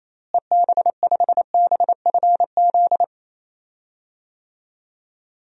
Morse code, heard as a keyed tone, about 27 wpm, 700 Hz. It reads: EB5BFZ
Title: DXCC-100 Call Signs - 6 Characters - Repeated 2x - 28wpm